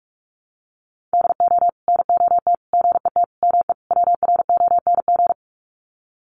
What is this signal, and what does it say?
Morse code, heard as a keyed tone, about 44 wpm, 700 Hz. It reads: DO NOT GET ME WRONG
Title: Sets of 5 Words Encore 45wpm